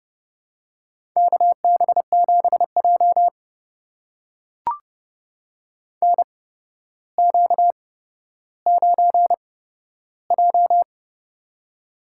Morse code, heard as a keyed tone, about 30 wpm, spaced as for 13 wpm, 700 Hz. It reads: KB7J NQ9J
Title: USA Call Signs - 4 Characters - 8x Character Spacing - 30wpm